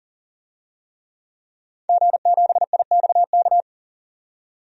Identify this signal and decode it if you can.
Morse code, heard as a keyed tone, about 40 wpm, 700 Hz. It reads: G7IXK
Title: DXCC-100 Call Signs - 5 Characters - 40wpm